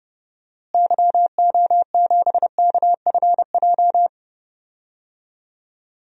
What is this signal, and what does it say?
Morse code, heard as a keyed tone, about 30 wpm, 700 Hz. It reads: YO7KFJ
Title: DXCC-100 Call Signs - 6 Characters - Repeated 2x - 30wpm